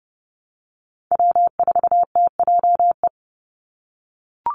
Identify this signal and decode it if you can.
Morse code, heard as a keyed tone, about 30 wpm, 700 Hz. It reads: W4TJE
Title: Call Signs - 30wpm